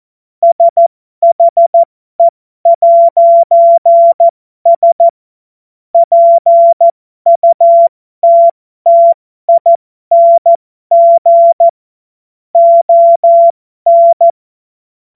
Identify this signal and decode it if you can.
Morse code, heard as a keyed tone, about 14 wpm, spaced as for 12 wpm, 680 Hz. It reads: SHE'S PUTTING ON
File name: Morsekurs / Witze